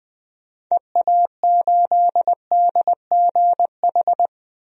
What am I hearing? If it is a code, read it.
Morse code, heard as a keyed tone, about 20 wpm, 700 Hz. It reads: EA8DGH
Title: DXCC-100 Call Signs - 6 Characters - 20wpm